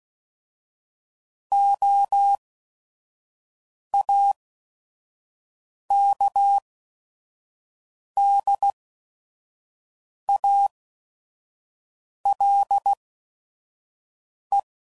Morse code, heard as a keyed tone, about 16 wpm, spaced as for 5 wpm, 800 Hz. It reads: OAKDALE
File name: CW Code Groups @5WPM / fromSM's / Cities